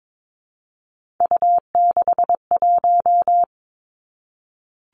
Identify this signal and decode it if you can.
Morse code, heard as a keyed tone, about 22 wpm, 700 Hz. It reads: U61
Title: Three Challenging Letter-Number Combinations 22wpm